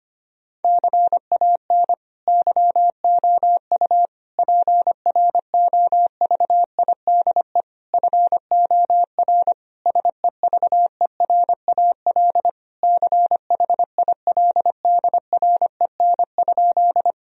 Morse code, heard as a keyed tone, about 25 wpm, 700 Hz. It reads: CAN YOU PROVIDE FOR SEVERAL CHILDREN?
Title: Sentences from Top 1000 Words - Review Part 2 - 25wpm